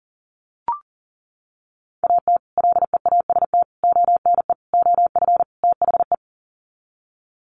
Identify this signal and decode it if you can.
Morse code, heard as a keyed tone, about 40 wpm, 700 Hz. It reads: AT LEAST ONE OF THE
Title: Sets of 5 Words Encore 40wpm